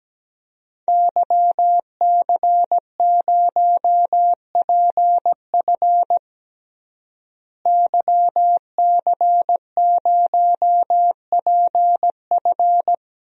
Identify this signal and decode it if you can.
Morse code, heard as a keyed tone, about 17 wpm, 700 Hz. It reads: YC0PF YC0PF
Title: DXCC-100 Call Signs - 5 Characters - Repeated 2x - 17wpm